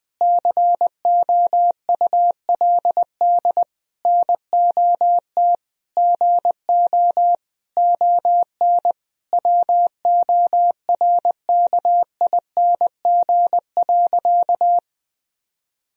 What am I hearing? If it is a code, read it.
Morse code, heard as a keyed tone, about 20 wpm, 700 Hz. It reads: COULD NOT GO ON WORKING.
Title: Sentences from Top 1000 Words - Review Part 4 - 20wpm